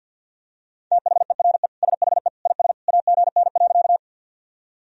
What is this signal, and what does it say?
Morse code, heard as a keyed tone, about 52 wpm, 700 Hz. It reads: THERE SHE IS AGN.
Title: Sentences from Top 200 Words and Abbreviations 50wpm